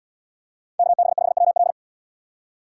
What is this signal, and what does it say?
Morse code, heard as a keyed tone, about 77 wpm, 700 Hz. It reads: KB5WL